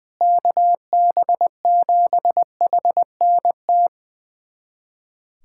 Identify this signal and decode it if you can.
Morse code, heard as a keyed tone, about 20 wpm, 700 Hz. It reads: KB7HNT
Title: USA Callsign DE Callsign - 6 Characters - 20wpm